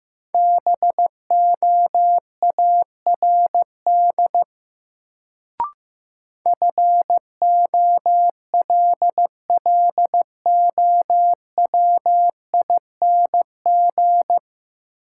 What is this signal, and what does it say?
Morse code, heard as a keyed tone, about 16 wpm, 700 Hz. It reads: BOARD FOLLOWING
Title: Sentences from Top 700 Words - New Word Drill - 15wpm